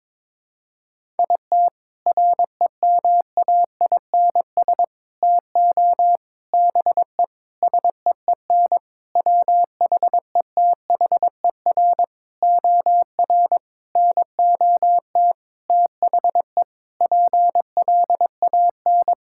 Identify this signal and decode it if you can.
Morse code, heard as a keyed tone, about 22 wpm, 700 Hz. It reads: IT REMAINS TO BE SEEN WHETHER OR NOT THE PLAN
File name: practice / 22wpm / Sentences with 8-Word Phrases - 22wpm